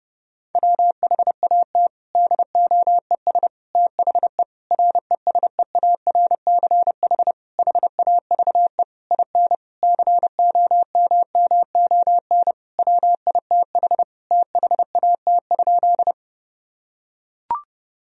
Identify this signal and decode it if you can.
Morse code, heard as a keyed tone, about 32 wpm, 700 Hz. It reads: WHAT DOES THE RESEARCH HAVE IN COMMON WITH THAT?
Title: Sentences from Top 1000 Words - Review Part 4 - 30wpm